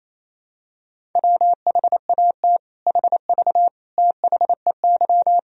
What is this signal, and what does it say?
Morse code, heard as a keyed tone, about 28 wpm, 700 Hz. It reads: WHAT HV THEY
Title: Sentences from Top 500 Words and Abbreviations 28wpm